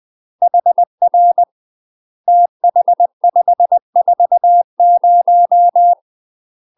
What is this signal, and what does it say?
Morse code, heard as a keyed tone, about 20 wpm, 700 Hz. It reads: HR TH540
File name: UK MASTER QSOs TRAINING / 20WPM / UK MASTER QSOs 0053